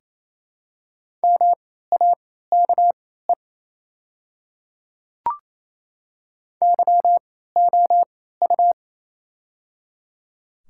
Morse code, heard as a keyed tone, about 27 wpm, spaced as for 16 wpm, 700 Hz. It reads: MAKE YOU
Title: 3x Character Spacing - Top 100 Words - 28wpm